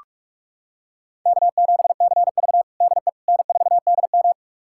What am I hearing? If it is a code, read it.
Morse code, heard as a keyed tone, about 42 wpm, 700 Hz. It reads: K7CV DE N4DM
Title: USA Callsign DE Callsign - 4 Characters - 45wpm